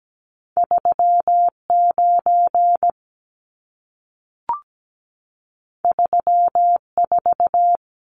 Morse code, heard as a keyed tone, about 17 wpm, 700 Hz. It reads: 39 34
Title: Numbers - 2-Digit 17wpm